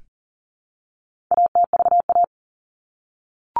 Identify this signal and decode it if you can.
Morse code, heard as a keyed tone, about 40 wpm, 700 Hz. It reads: ATVA
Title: All Cut Numbers - 4-Digit 40wpm